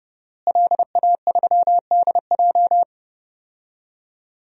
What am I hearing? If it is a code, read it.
Morse code, heard as a keyed tone, about 30 wpm, 700 Hz. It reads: LA3DJ
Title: DXCC-100 Call Signs - 5 Characters - Repeated 2x - 30wpm